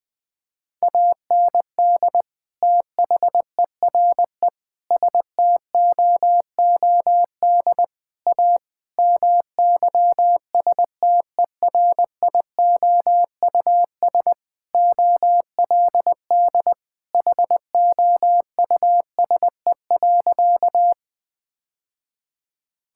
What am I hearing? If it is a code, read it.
Morse code, heard as a keyed tone, about 21 wpm, 700 Hz. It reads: AND THERE STOOD A MYSTERIOUS OLD HOUSE.
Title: Sentences with 2-Word Phrases - 20wpm